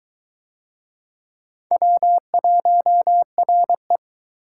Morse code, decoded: W1RE

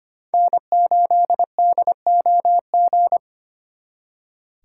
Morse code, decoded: N8DOG

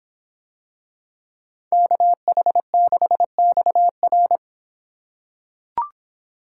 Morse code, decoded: KH6XR

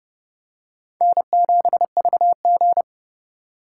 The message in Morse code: N7VG

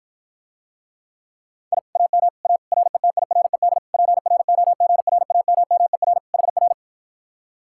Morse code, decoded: I AM A RETIRED PROGRAMMER HR

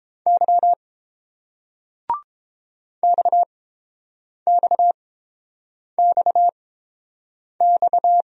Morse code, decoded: Y XXXX

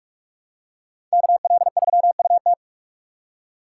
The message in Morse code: KL3UT

